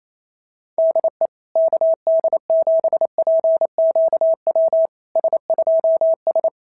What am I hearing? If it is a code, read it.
Morse code, heard as a keyed tone, about 28 wpm, 650 Hz. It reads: DE KD7PQW S2S
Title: SOTA QSO Practice 28wpm